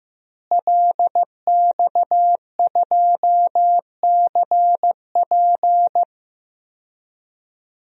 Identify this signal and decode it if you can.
Morse code, heard as a keyed tone, about 15 wpm, 700 Hz. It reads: LX2CP
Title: DXCC-100 Call Signs - 5 Characters - Repeated 3x - 15wpm